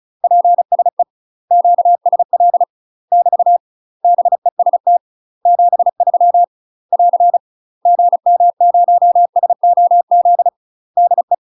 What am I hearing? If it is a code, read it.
Morse code, heard as a keyed tone, about 35 wpm, 700 Hz. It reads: PSE QSL <BT> BEST 73 <AR> GM0SOZ DE
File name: UK MASTER QSOs TRAINING / 35WPM / UK MASTER QSOs 0025